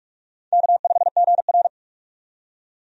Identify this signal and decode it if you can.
Morse code, heard as a keyed tone, about 42 wpm, 700 Hz. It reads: K5GR
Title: USA Call Signs - 4 Characters - 45wpm